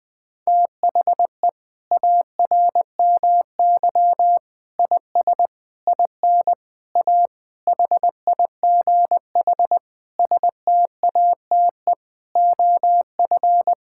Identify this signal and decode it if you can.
Morse code, heard as a keyed tone, about 20 wpm, 700 Hz. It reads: THE ARMY IS IN A HIGH STATE OF
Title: Sentences from Top 1000 Words - Review Part 6 - 20wpm